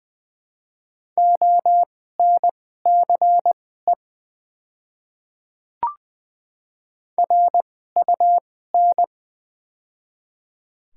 Morse code, heard as a keyed tone, about 20 wpm, spaced as for 14 wpm, 700 Hz. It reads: ONCE RUN
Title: 2x Character Spacing - Top 300 Words - 20wpm